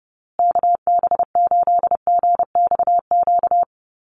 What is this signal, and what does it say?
Morse code, heard as a keyed tone, about 30 wpm, 700 Hz. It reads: KB8GXQ
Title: USA Call Signs Repeated 2x 30wpm